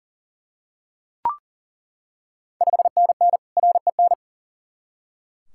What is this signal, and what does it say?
Morse code, heard as a keyed tone, about 40 wpm, 700 Hz. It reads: 5NN REN
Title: Ontario QSO Party Exchange 40wpm